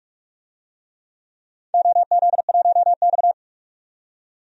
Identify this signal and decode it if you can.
Morse code, heard as a keyed tone, about 46 wpm, 700 Hz. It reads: OZ1X
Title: DXCC-100 Call Signs - 4 Characters - 45wpm